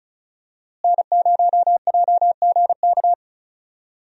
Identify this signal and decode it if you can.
Morse code, heard as a keyed tone, about 36 wpm, 700 Hz. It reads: N0JGK
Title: USA Call Signs - 5 Characters - Repeated 2x - 35wpm